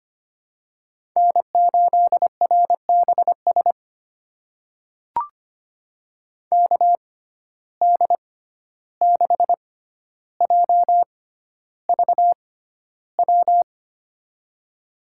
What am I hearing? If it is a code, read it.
Morse code, heard as a keyed tone, about 25 wpm, spaced as for 12 wpm, 700 Hz. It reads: N8RBS KD6JVW
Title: USA Call Signs - 6x Character Spacing - 25wpm